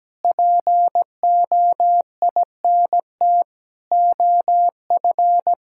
Morse code, heard as a keyed tone, about 17 wpm, 700 Hz. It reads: POINT OF
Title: Sentences from Top 1000 Words - Review Part 6 - 17wpm